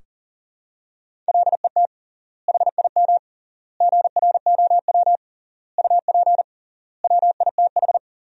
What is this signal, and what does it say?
Morse code, heard as a keyed tone, about 40 wpm, 700 Hz. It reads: LET HIM GROW UP WITH
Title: 3x Word-Spacing - Sentences from Top 300 Words - 40wpm